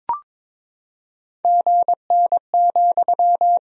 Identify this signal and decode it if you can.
Morse code, heard as a keyed tone, about 22 wpm, 700 Hz. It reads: GN,
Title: Sentences from Top 300 Words and Abbreviations 22wpm